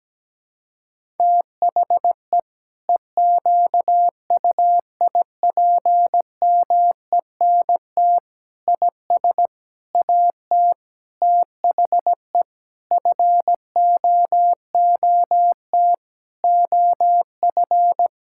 Morse code, heard as a keyed tone, about 17 wpm, 700 Hz. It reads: THE EQUIPMENT IS AT THE FOOT OF
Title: Sentences from Top 1000 Words - Review Part 5 - 17wpm